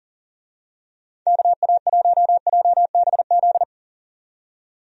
Morse code, heard as a keyed tone, about 37 wpm, 700 Hz. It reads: KA1JBZ
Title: USA Call Signs - 6 Characters - 40wpm